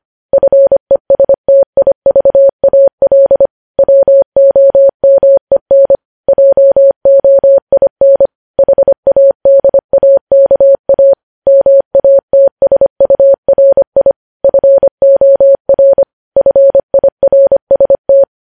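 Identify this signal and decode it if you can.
Morse code, heard as a keyed tone, about 25 wpm, 550 Hz. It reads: FESTIVAL WOMEN JOIN HADAKA MATSURI FOR FIRST
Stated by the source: Morse code practice of news headlines at 25 wpm, opening with vvv.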